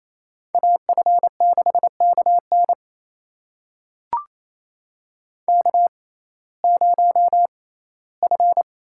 Morse code, heard as a keyed tone, about 28 wpm, spaced as for 17 wpm, 700 Hz. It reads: AF6KN K0F
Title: USA Call Signs - 6x Character Spacing - 28wpm